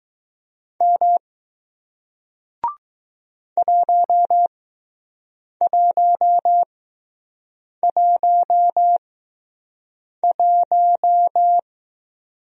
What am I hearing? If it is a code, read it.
Morse code, heard as a keyed tone, about 20 wpm, spaced as for 6 wpm, 700 Hz. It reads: M 1111